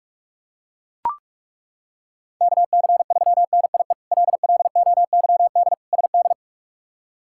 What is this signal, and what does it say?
Morse code, heard as a keyed tone, about 42 wpm, 700 Hz. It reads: KC3NIE LLOYD SD
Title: K1USN SST QSO Practice - 45wpm